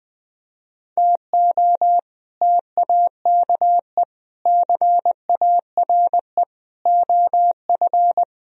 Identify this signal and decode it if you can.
Morse code, heard as a keyed tone, about 21 wpm, 700 Hz. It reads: TO TAKE CARE OF
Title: Speed-Racing - Sets of 4 Words 20wpm